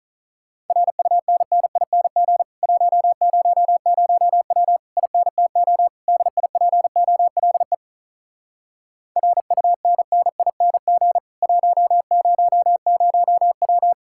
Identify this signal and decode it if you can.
Morse code, heard as a keyed tone, about 48 wpm, 700 Hz. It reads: RUNNING 100W INTO DIPOLE RUNNING 100W